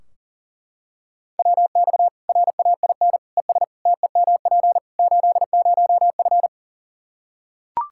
Morse code, heard as a keyed tone, about 40 wpm, 700 Hz. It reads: WX RAIN ES TEMP 80F
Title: K1USN SST QSO Practice 40wpm